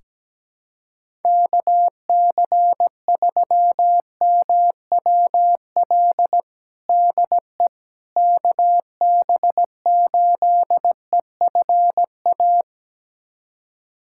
Morse code, heard as a keyed tone, about 17 wpm, 700 Hz. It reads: KC3MWL DE KB8EFA
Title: USA Callsign DE Callsign - 6 Characters - 17wpm